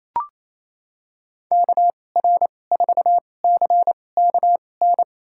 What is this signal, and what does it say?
Morse code, heard as a keyed tone, about 28 wpm, spaced as for 20 wpm, 700 Hz. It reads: KR4CKN
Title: USA Call Signs - 2x Character Spacing - 28wpm